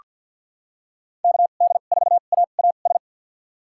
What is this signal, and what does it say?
Morse code, heard as a keyed tone, about 48 wpm, spaced as for 35 wpm, 700 Hz. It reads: KD4AAS
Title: USA Call Signs - 2x Character Spacing - 50wpm